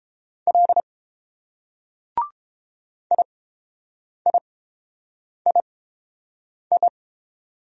Morse code, heard as a keyed tone, about 30 wpm, spaced as for 7 wpm, 700 Hz. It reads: L IIII